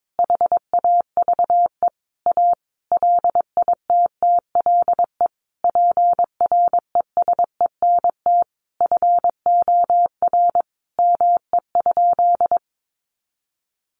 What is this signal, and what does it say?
Morse code, heard as a keyed tone, about 22 wpm, 700 Hz. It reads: HAVE A LITTLE PRESENT FOR ME?
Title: Sentences from Top 800 Words 22wpm